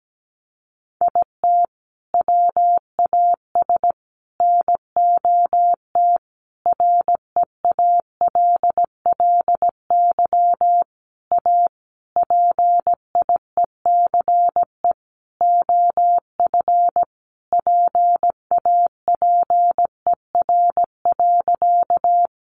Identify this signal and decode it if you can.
Morse code, heard as a keyed tone, about 17 wpm, 700 Hz. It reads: IT WAS NOT REALLY A PIECE OF PAPER.
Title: Sentences from Top 600 Words 17wpm